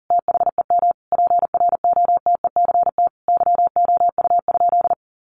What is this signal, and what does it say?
Morse code, heard as a keyed tone, about 37 wpm, 700 Hz. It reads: THEM PROTECT YOU?
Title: Sentences from Top 900 Words 40wpm